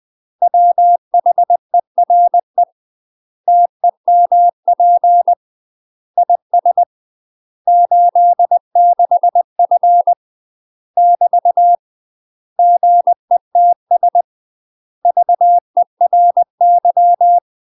Morse code, heard as a keyed tone, about 20 wpm, 700 Hz. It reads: WHERE TEMP IS 86F <BT> GETS VERY
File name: UK MASTER QSOs TRAINING / 20WPM / UK MASTER QSOs 0035